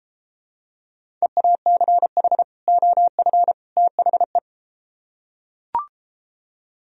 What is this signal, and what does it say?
Morse code, heard as a keyed tone, about 32 wpm, 700 Hz. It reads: EACH OF THE